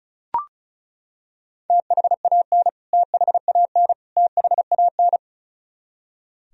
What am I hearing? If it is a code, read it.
Morse code, heard as a keyed tone, about 36 wpm, 700 Hz. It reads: THAN THAN THAN
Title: Top 100 Words Repeated 35wpm